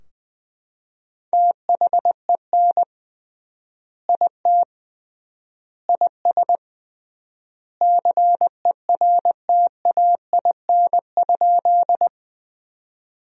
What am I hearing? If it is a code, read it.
Morse code, heard as a keyed tone, about 20 wpm, 700 Hz. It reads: THEN IT IS CERTAIN?
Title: 3x Word-Spacing - Sentences from Top 500 Words - 20wpm